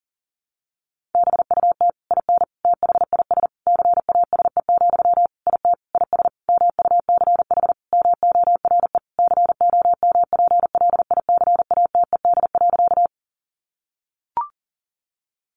Morse code, heard as a keyed tone, about 40 wpm, 700 Hz. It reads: BUT IN THIS CASE, IT IS MUCH MORE COMPLICATED.
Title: Sentences from Top 1200 Words 40wpm